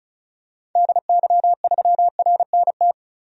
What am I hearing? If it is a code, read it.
Morse code, heard as a keyed tone, about 36 wpm, 700 Hz. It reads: DY3RNT
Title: DXCC-100 Call Signs - 6 Characters - 35wpm